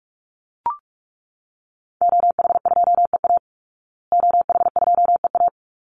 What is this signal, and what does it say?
Morse code, heard as a keyed tone, about 44 wpm, 700 Hz. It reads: OH2EA OH2EA
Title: Call Signs Repeated 3x - 45wpm